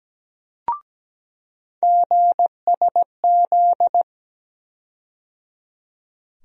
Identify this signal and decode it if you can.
Morse code, heard as a keyed tone, about 17 wpm, 700 Hz. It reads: GSZ
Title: USA Call Sign Postfixes - 3 Letters 17wpm